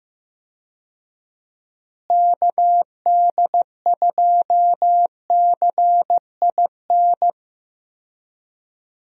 Morse code, heard as a keyed tone, about 16 wpm, 700 Hz. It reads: KD2CIN
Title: USA Call Signs - 6 Characters - Repeated 3x - 15wpm